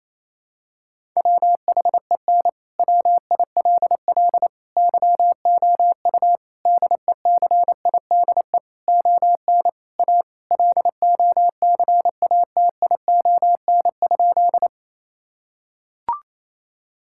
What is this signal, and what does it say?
Morse code, heard as a keyed tone, about 28 wpm, 700 Hz. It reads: WHEN WILL YOU DECIDE ON A LOCATION?
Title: Sentences from Top 1000 Words - Review Part 6 - 28wpm